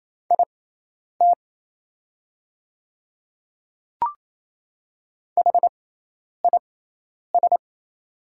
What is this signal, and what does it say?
Morse code, heard as a keyed tone, about 28 wpm, spaced as for 10 wpm, 700 Hz. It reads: IT HIS
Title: 6x Character Spacing - Top 100 Words - 28wpm